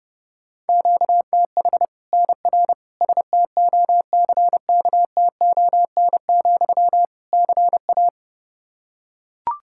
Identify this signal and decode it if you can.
Morse code, heard as a keyed tone, about 30 wpm, 700 Hz. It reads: QTH NR STOCKTON, CA
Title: Practice QSO - Exchange 1 - Beginning - 30wpm